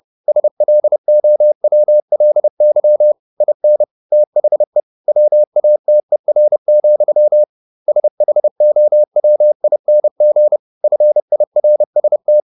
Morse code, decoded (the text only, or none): SLOWLY IN THE WATER, SHOWING FIRST